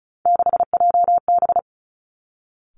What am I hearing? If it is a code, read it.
Morse code, heard as a keyed tone, about 36 wpm, 700 Hz. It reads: 6JB
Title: Three Challenging Letter-Number Combinations 35wpm